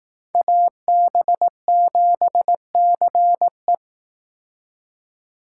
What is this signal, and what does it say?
Morse code, heard as a keyed tone, about 18 wpm, 700 Hz. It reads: AB7CE
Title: Speed Racing - USA 2x2 Call Signs 15wpm